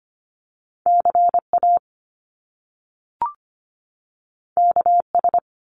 Morse code, heard as a keyed tone, about 25 wpm, 700 Hz. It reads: CA KS
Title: US States 25wpm